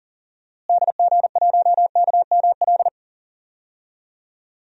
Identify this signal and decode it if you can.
Morse code, heard as a keyed tone, about 40 wpm, 700 Hz. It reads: DG1KML
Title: DXCC-100 Call Signs - 6 Characters - Repeated 3x - 40wpm